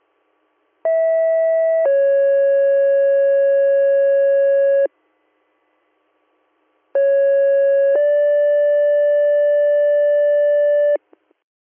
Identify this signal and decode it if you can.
Two-tone page, heard as a one-second tone, then a three-second tone, about 650 then 550 Hz, about 550 then 580 Hz. It is heard from 0.8 to 11.0 s.